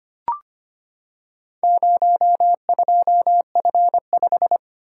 Morse code, heard as a keyed tone, about 25 wpm, 700 Hz. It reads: 02F5
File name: practice / 25wpm / Four Letter-Number Combinations - 25wpm